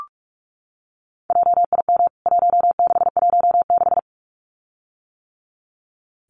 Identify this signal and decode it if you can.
Morse code, heard as a keyed tone, about 44 wpm, 700 Hz. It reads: JIM 1616
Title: CWT Exchange 45wpm